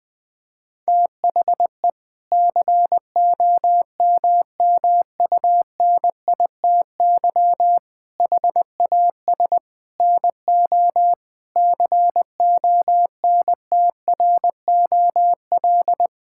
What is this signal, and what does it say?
Morse code, heard as a keyed tone, about 21 wpm, 700 Hz. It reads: THE COMMUNITY HAS NO CONTROL
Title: Sentences from Top 1000 Words - Review Part 5 - 20wpm